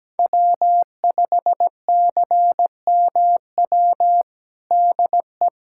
Morse code, heard as a keyed tone, about 17 wpm, 700 Hz. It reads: W5CMW DE
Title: USA Callsign DE Callsign 17wpm